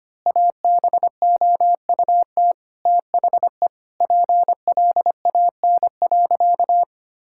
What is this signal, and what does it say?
Morse code, heard as a keyed tone, about 25 wpm, 700 Hz. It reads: ABOUT THE PLAN.
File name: practice / 25wpm / Sentences with 4-Word Phrases - 25wpm